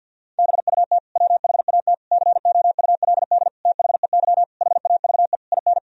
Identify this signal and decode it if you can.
Morse code, heard as a keyed tone, about 50 wpm, 700 Hz. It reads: BUT WHAT COULD THEY HAVE IN
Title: Sentences from Top 1000 Words - Review Part 3 - 50wpm